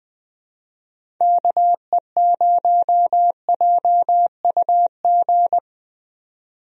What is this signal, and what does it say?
Morse code, heard as a keyed tone, about 20 wpm, 700 Hz. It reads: KE0JUG
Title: USA Call Signs - 6 Characters - 20wpm